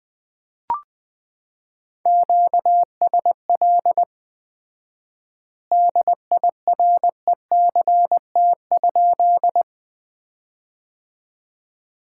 Morse code, heard as a keyed tone, about 20 wpm, 700 Hz. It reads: QSL DIRECT?
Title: Common QSO Phrases - 4x Word-Spacing - 20wpm